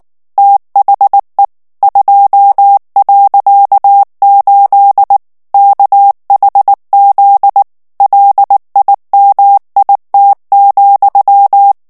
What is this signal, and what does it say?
Morse code, heard as a keyed tone, about 19 wpm, 800 Hz. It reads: THE 2.8 KHZ LIMIT,
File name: ANGLAIS texte clair 3 vitesse 18 mots minute